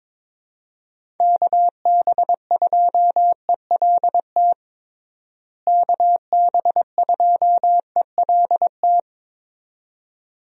Morse code, heard as a keyed tone, about 22 wpm, 700 Hz. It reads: KB2ELT KB2ELT
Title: USA Call Signs - 6 Characters - Repeated 2x - 22wpm